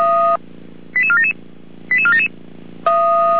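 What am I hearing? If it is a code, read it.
Five-tone selective call (ZVEI), heard as quick runs of five tones, tones of about 70 ms. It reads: 88477 88477
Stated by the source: ZVEI five-tone selective call received with an SDR.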